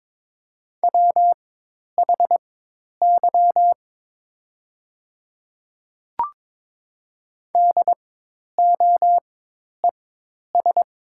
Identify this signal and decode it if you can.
Morse code, heard as a keyed tone, about 22 wpm, spaced as for 10 wpm, 700 Hz. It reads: WHY DOES
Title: 4x Character Spacing - Top 500 Words - 22wpm